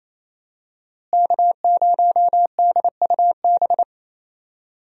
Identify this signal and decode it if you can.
Morse code, heard as a keyed tone, about 28 wpm, 700 Hz. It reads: K0DUB
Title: USA Call Signs - 5 Characters - Repeated 2x - 28wpm